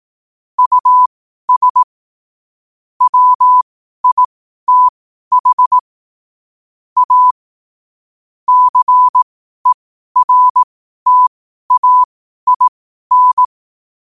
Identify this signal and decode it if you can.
Morse code, heard as a keyed tone, about 18 wpm, spaced as for 12 wpm, 1000 Hz. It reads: US WITH A CERTAIN
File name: qotdfiles / 12wpm